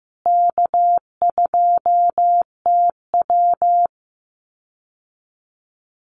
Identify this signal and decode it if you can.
Morse code, heard as a keyed tone, about 15 wpm, 700 Hz. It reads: K2TW
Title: Easy Call Signs Repeated 3x - 15wpm